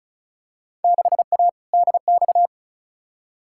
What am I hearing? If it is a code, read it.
Morse code, heard as a keyed tone, about 36 wpm, 700 Hz. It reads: 6A DX
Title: Field Day Exchanges Encore - 35wpm